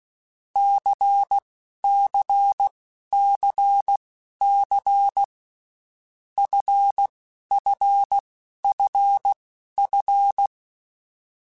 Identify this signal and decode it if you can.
Morse code, heard as a keyed tone, about 16 wpm, spaced as for 11 wpm, 800 Hz. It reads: CCCC FFFF
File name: Groups@10 / Groups@10 mp3s / LFCYQ@10